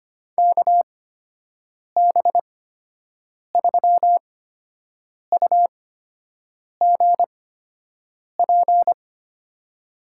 Morse code, heard as a keyed tone, about 25 wpm, spaced as for 7 wpm, 700 Hz. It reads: KB3UGP
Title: USA Call Signs - 6 Characters - 8x Character Spacing - 25wpm